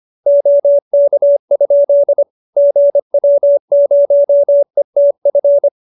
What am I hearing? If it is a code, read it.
Morse code, heard as a keyed tone, about 25 wpm, 570 Hz. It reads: OK? GW0ETF
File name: QSO318 25